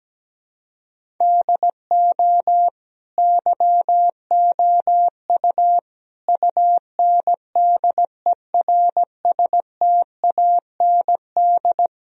Morse code, heard as a keyed tone, about 17 wpm, 700 Hz. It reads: DO YOU UNDERSTAND
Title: Sentences from Top 1000 Words - Review Part 3 - 17wpm